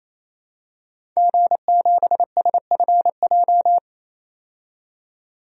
Morse code, heard as a keyed tone, about 28 wpm, 700 Hz. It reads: G7SFJ
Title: DXCC-100 Call Signs - 5 Characters - 28wpm